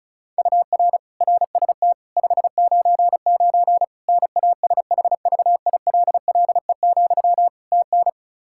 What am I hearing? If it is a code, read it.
Morse code, heard as a keyed tone, about 36 wpm, 700 Hz. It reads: UR RST 599 NASHVILLE, TN
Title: Practice QSO - Exchange 1 - Beginning - 35wpm